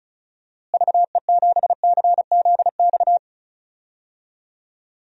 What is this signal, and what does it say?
Morse code, heard as a keyed tone, about 34 wpm, 700 Hz. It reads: VE7CZX
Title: DXCC-100 Call Signs - 6 Characters - Repeated 2x - 35wpm